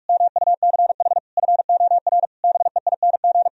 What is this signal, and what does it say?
Morse code, heard as a keyed tone, about 44 wpm, 700 Hz. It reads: MUCH FOR BEING